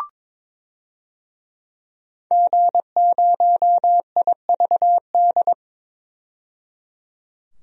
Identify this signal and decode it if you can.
Morse code, heard as a keyed tone, about 21 wpm, 700 Hz. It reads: G0IVD
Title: DXCC-100 Call Signs - 5 Characters - 22wpm